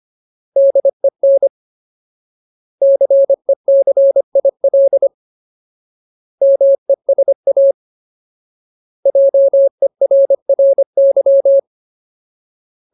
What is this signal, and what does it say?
Morse code, heard as a keyed tone, about 25 wpm, 550 Hz. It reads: DEN CECIL MESA JERRY
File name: Ultimate Word List / 25WPM Ultimate